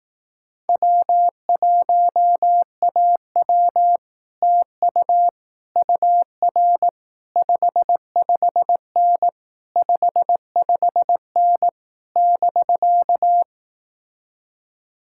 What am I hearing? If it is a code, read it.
Morse code, heard as a keyed tone, about 18 wpm, 700 Hz. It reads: W1AW TU UR 55N 55N <BK>
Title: Speed-Racing Common QSO Phrases Encore - 15wpm